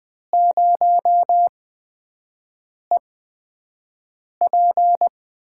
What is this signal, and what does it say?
Morse code, heard as a keyed tone, about 20 wpm, spaced as for 5 wpm, 700 Hz. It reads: 0EP
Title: USA Call Signs - 6 Characters - 8x Character Spacing - 20wpm